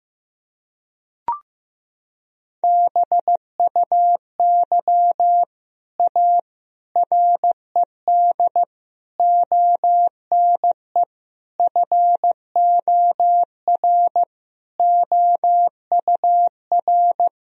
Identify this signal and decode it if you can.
Morse code, heard as a keyed tone, about 15 wpm, 700 Hz. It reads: BUY A RED ONE FOR OUR
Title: Sentences from Top 1000 Words - Review Part 2 - 15wpm